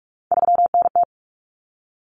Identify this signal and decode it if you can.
Morse code, heard as a keyed tone, about 44 wpm, 700 Hz. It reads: 3NT
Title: Common Cut Numbers - 3-Digit 45wpm